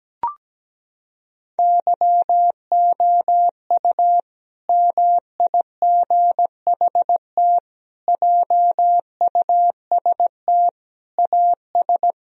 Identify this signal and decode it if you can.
Morse code, heard as a keyed tone, about 17 wpm, 700 Hz. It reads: YOU MIGHT JUST AS
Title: Sentences from Top 1000 Words - Review Part 2 - 17wpm